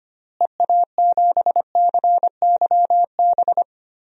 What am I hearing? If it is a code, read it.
Morse code, heard as a keyed tone, about 25 wpm, 700 Hz. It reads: EA7CYB
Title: DXCC-100 Call Signs - 6 Characters - Repeated 3x - 25wpm